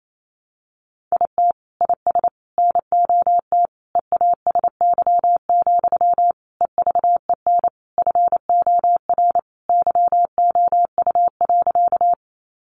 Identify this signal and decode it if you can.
Morse code, heard as a keyed tone, about 28 wpm, 700 Hz. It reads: IT IS NOT EASY, EVEN FOR YOU.
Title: Sentences from Top 700 Words 28wpm